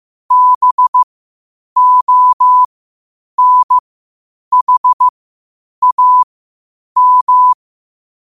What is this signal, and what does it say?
Morse code, heard as a keyed tone, about 15 wpm, spaced as for 8 wpm, 1000 Hz. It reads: BONHAM
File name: qotdfiles / 8wpm